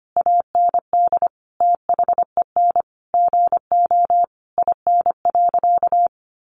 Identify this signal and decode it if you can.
Morse code, heard as a keyed tone, about 25 wpm, 700 Hz. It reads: AND THEN GO IN.
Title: Sentences from Top 700 Words 25wpm